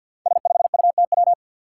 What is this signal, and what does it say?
Morse code, heard as a keyed tone, about 50 wpm, 700 Hz. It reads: S5UTW
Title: Five Challenging Letter-Number Combinations 50wpm